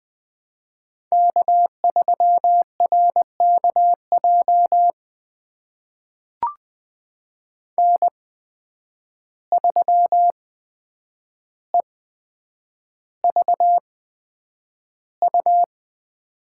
Morse code, heard as a keyed tone, about 20 wpm, spaced as for 9 wpm, 700 Hz. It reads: K3RKJ N3EVU
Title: USA Call Signs - 5 Characters - 8x Character Spacing - 20wpm